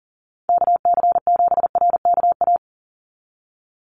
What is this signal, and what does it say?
Morse code, heard as a keyed tone, about 40 wpm, 700 Hz. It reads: KC7RKA
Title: USA Call Signs 40wpm